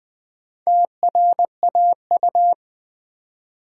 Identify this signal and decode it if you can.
Morse code, heard as a keyed tone, about 20 wpm, 700 Hz. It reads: TRAU
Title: Build A Word - 2 Letters At A Time - 6-Letter Words - 20wpm